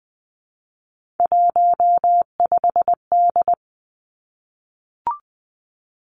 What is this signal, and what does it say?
Morse code, heard as a keyed tone, about 20 wpm, 700 Hz. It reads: 15D
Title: Three Challenging Letter-Number Combinations 20wpm